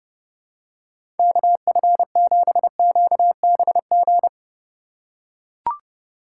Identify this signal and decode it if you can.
Morse code, heard as a keyed tone, about 30 wpm, 700 Hz. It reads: KF7QBG
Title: USA Call Signs - 6 Characters - 4x Character Spacing - 30wpm